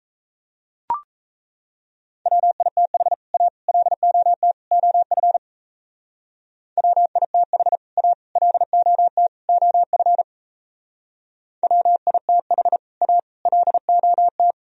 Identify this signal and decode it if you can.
Morse code, heard as a keyed tone, about 33 wpm, 700 Hz. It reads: WITH A LOT OF WITH A LOT OF WITH A LOT
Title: Speed-Racing - Sets of 4 Words 28wpm